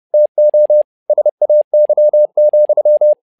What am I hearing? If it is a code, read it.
Morse code, heard as a keyed tone, about 30 wpm, 600 Hz. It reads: TO SAY,
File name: learning center / tower trouble / tower trouble 30wpm